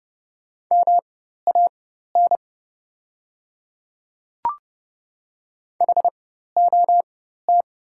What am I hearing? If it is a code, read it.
Morse code, heard as a keyed tone, about 30 wpm, spaced as for 14 wpm, 700 Hz. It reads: MAN HOT
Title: 4x Character Spacing - Top 200 Words - 30wpm